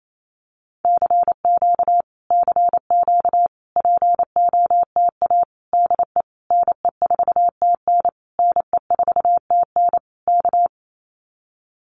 Morse code, heard as a keyed tone, about 28 wpm, 700 Hz. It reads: CQ CQ POTA DE NE4TN NE4TN K
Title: Common QSO Phrases Encore 28wpm